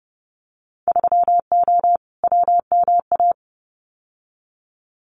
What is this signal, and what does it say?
Morse code, heard as a keyed tone, about 30 wpm, 700 Hz. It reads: 3O WMA
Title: Winter Field Day Exchanges Encore 30wpm